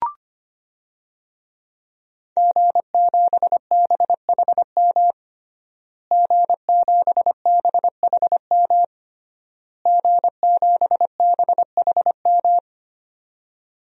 Morse code, heard as a keyed tone, about 25 wpm, 700 Hz. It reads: G7BHM G7BHM G7BHM
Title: DXCC-100 Call Signs - 5 Characters - Repeated 3x - 25wpm